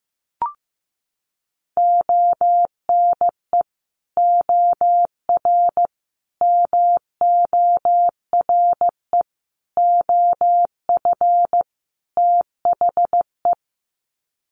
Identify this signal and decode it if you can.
Morse code, heard as a keyed tone, about 15 wpm, 700 Hz. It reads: ONE OR MORE OF THE
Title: Sets of 5 Words Encore 15wpm